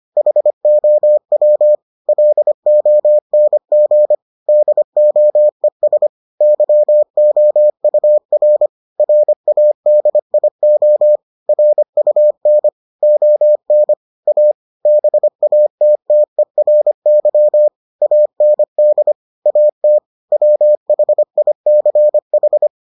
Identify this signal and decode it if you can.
Morse code, heard as a keyed tone, about 25 wpm, 600 Hz. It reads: HOW LONG DOES YOUR RADIO RUN ON A BATTERY AND AT WHICH